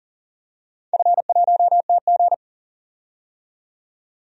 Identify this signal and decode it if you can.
Morse code, heard as a keyed tone, about 40 wpm, 700 Hz. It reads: F1TG
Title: DXCC-100 Call Signs - 4 Characters - 40wpm